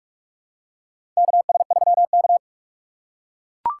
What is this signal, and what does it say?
Morse code, heard as a keyed tone, about 41 wpm, 700 Hz. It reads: KS3K
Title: USA Call Signs - 4 Characters - 4x Character Spacing - 45wpm